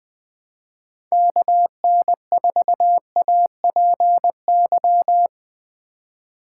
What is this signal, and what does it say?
Morse code, heard as a keyed tone, about 20 wpm, 700 Hz. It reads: KN4APY
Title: USA Call Signs - 6 Characters - 2x Character Spacing - 20wpm